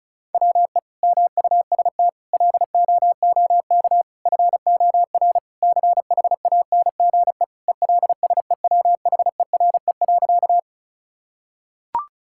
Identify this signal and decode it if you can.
Morse code, heard as a keyed tone, about 36 wpm, 700 Hz. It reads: WE MUST LOOK FOR CHANGE ELSEWHERE.
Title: Sentences from Top 1600 Words 35wpm